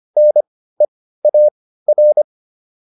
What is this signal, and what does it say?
Morse code, heard as a keyed tone, about 25 wpm, spaced as for 15 wpm, 600 Hz. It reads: NEAR